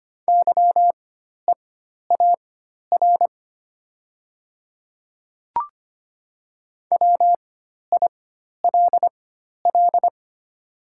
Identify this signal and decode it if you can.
Morse code, heard as a keyed tone, about 25 wpm, spaced as for 12 wpm, 700 Hz. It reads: YEAR WILL
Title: 4x Character Spacing - Top 200 Words - 25wpm